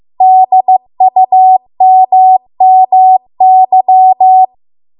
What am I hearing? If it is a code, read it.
Morse code, heard as a keyed tone, about 15 wpm, 750 Hz. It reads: DUMMY